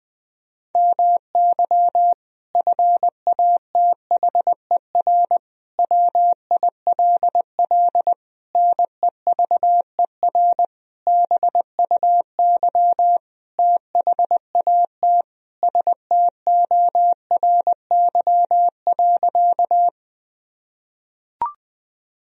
Morse code, heard as a keyed tone, about 20 wpm, 700 Hz. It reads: MY FATHER WILL NEVER BUY THAT STORY.
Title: Sentences from Top 1000 Words - Review Part 4 - 20wpm